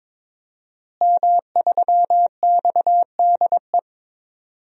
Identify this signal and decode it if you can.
Morse code, heard as a keyed tone, about 22 wpm, 700 Hz. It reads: M3XDE
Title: DXCC-100 Call Signs - 5 Characters - Repeated 2x - 22wpm